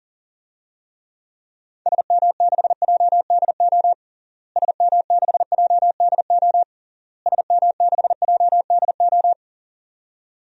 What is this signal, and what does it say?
Morse code, heard as a keyed tone, about 40 wpm, 700 Hz. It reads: SM6JDO SM6JDO SM6JDO
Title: DXCC-100 Call Signs - 6 Characters - Repeated 3x - 40wpm